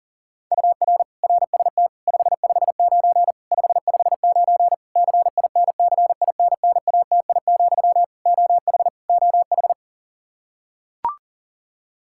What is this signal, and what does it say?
Morse code, heard as a keyed tone, about 40 wpm, 700 Hz. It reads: UR RST 559 559 CINCINNATI, OH OH
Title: Practice QSO - Exchange 1 - Beginning - 40wpm